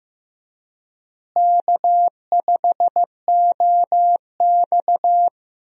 Morse code, heard as a keyed tone, about 15 wpm, 700 Hz. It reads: K5OX
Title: USA Call Signs - 4 Characters - Repeated 3x - 15wpm